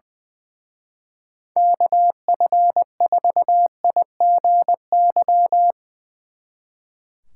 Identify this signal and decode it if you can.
Morse code, heard as a keyed tone, about 22 wpm, 700 Hz. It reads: KF4IGY
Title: Speed Racing - USA 2x3 Call Signs 20wpm